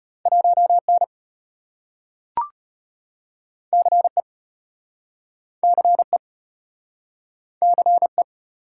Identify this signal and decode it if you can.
Morse code, heard as a keyed tone, about 34 wpm, 700 Hz. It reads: 1N CE CE CE